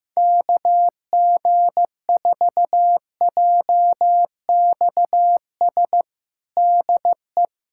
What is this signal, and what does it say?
Morse code, heard as a keyed tone, about 15 wpm, 700 Hz. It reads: KG4JXS DE
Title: USA Callsign DE Callsign - 6 Characters - 15wpm